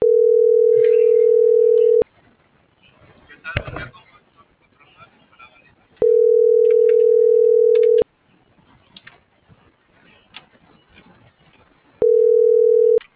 Ringback tone (North America), 440 + 480 Hz.